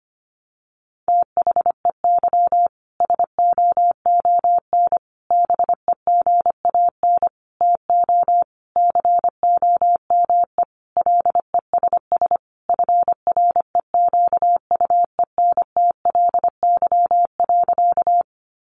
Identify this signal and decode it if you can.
Morse code, heard as a keyed tone, about 25 wpm, 700 Hz. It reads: THEY SOON BEGAN TO COME LESS FREQUENTLY.
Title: Sentences from Top 1300 Words 25wpm